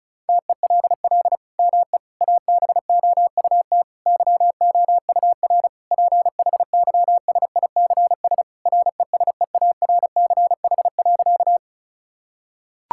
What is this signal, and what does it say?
Morse code, heard as a keyed tone, about 36 wpm, 700 Hz. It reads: TELL ME ABOUT YOUR PHYSICS RESEARCH.
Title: Sentences from Top 1000 Words - Review Part 5 - 35wpm